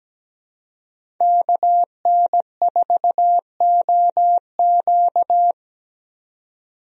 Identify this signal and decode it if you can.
Morse code, heard as a keyed tone, about 17 wpm, 700 Hz. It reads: KN4OQ
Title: USA Call Signs - 5 Characters - 17wpm